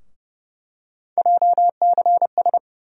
Morse code, heard as a keyed tone, about 31 wpm, 700 Hz. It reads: JCS
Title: USA Call Sign Postfixes - 3 Letters 30wpm